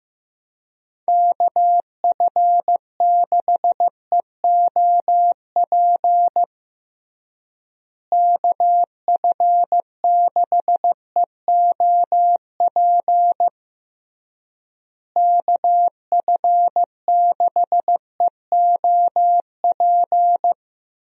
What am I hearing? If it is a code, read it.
Morse code, heard as a keyed tone, about 16 wpm, 700 Hz. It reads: KF6EOP KF6EOP KF6EOP
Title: USA Call Signs - 6 Characters - Repeated 3x - 15wpm